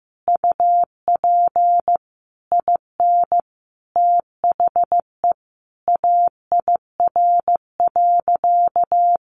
Morse code, heard as a keyed tone, about 15 wpm, 700 Hz. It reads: UP IN THE AIR.
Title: Easy Phrases 15wpm